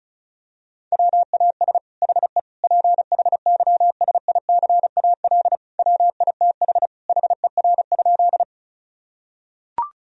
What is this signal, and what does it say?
Morse code, heard as a keyed tone, about 34 wpm, 700 Hz. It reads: WAS HE PHYSICAL WITH HER?
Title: Sentences from Top 1000 Words - Review Part 4 - 35wpm